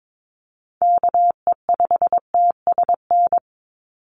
Morse code, heard as a keyed tone, about 22 wpm, 700 Hz. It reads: KE5TSN
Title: USA Call Signs 22wpm